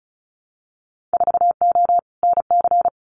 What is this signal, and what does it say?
Morse code, heard as a keyed tone, about 34 wpm, 700 Hz. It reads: 4O NC
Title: Winter Field Day Exchanges 35wpm